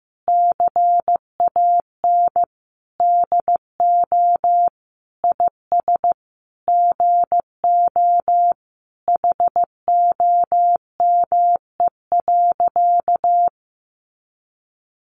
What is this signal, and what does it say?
Morse code, heard as a keyed tone, about 15 wpm, 700 Hz. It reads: CAN DO IS GO HOME.
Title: Sentences from Top 200 Words 15wpm